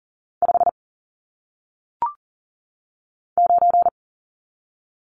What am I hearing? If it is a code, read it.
Morse code, heard as a keyed tone, about 39 wpm, 700 Hz. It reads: 5 9
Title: Numbers - 1-Digit 40wpm